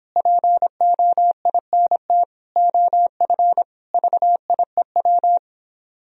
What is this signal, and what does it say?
Morse code, heard as a keyed tone, about 26 wpm, 700 Hz. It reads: POINT OF VIEW